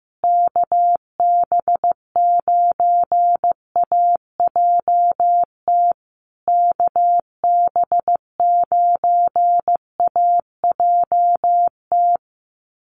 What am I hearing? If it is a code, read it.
Morse code, heard as a keyed tone, about 14 wpm, 700 Hz. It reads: KB9AJT KB9AJT
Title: USA Call Signs Repeated 3x 15wpm